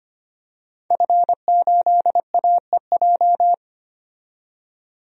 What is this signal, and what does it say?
Morse code, heard as a keyed tone, about 25 wpm, 700 Hz. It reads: F8AEJ
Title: DXCC-100 Call Signs - 5 Characters - 25wpm